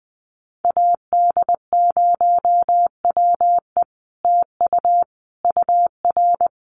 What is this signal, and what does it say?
Morse code, heard as a keyed tone, about 20 wpm, 700 Hz. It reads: AD0WE TU UR
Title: Common QSO Phrases Encore 20wpm